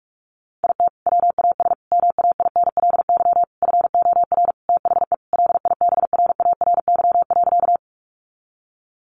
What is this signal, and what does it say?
Morse code, heard as a keyed tone, about 44 wpm, 700 Hz. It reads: IT WAS MAINLY FOR THE LIBRARY.
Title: Sentences from Top 1400 Words 45wpm